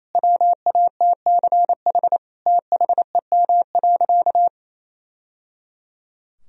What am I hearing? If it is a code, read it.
Morse code, heard as a keyed tone, about 28 wpm, 700 Hz. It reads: WATCH THEM.
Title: Sentences from Top 1000 Words - Review Part 1 - 28wpm